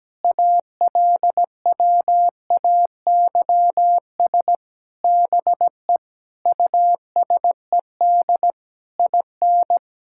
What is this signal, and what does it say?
Morse code, heard as a keyed tone, about 17 wpm, 700 Hz. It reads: ALWAYS BE USED IN
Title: Sentences from Top 1000 Words - Review Part 6 - 17wpm